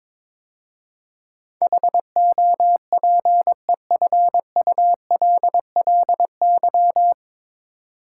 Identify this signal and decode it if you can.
Morse code, heard as a keyed tone, about 23 wpm, 700 Hz. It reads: HOPEFULLY
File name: 9 Letter Words - First 3 Letters Given - 22wpm